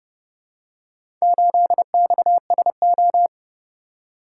Morse code, heard as a keyed tone, about 29 wpm, 700 Hz. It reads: 8XSO